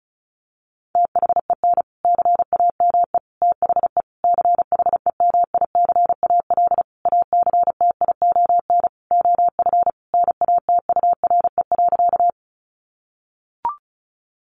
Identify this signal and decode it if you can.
Morse code, heard as a keyed tone, about 34 wpm, 700 Hz. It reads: THEN CAME THE CHEMICAL ACTION OF NATURE.
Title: Sentences from Top 1200 Words 35wpm